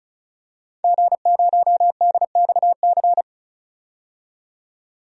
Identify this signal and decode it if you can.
Morse code, heard as a keyed tone, about 36 wpm, 700 Hz. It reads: G0DXC
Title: DXCC-100 Call Signs - 5 Characters - 35wpm